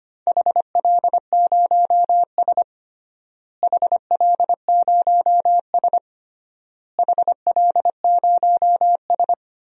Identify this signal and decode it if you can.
Morse code, heard as a keyed tone, about 25 wpm, 700 Hz. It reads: HL0S HL0S HL0S
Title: DXCC-100 Call Signs - 4 Characters - Repeated 3x - 25wpm